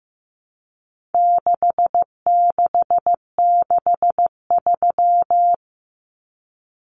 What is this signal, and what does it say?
Morse code, heard as a keyed tone, about 14 wpm, 700 Hz. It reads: 6663
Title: Numbers - 4-Digit 15wpm